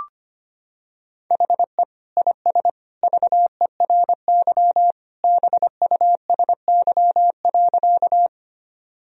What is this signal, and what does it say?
Morse code, heard as a keyed tone, about 25 wpm, 700 Hz. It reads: HE IS VERY BUSY.
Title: Sentences from Top 1000 Words - Review Part 3 - 25wpm